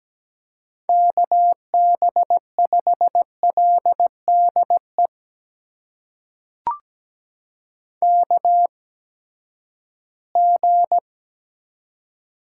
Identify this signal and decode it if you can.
Morse code, heard as a keyed tone, about 17 wpm, spaced as for 12 wpm, 700 Hz. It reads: KB5LDE KG
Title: USA Call Signs - 8x Character Spacing - 17wpm